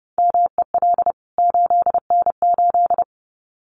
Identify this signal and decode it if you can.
Morse code, heard as a keyed tone, about 30 wpm, 700 Hz. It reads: MEL 8N8
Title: CWT Exchange 30wpm